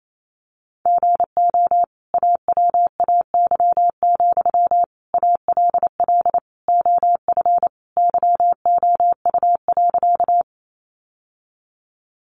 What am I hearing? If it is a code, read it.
Morse code, heard as a keyed tone, about 28 wpm, 700 Hz. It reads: GO AWAY, ALL OF YOU.
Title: Sentences from Top 600 Words 28wpm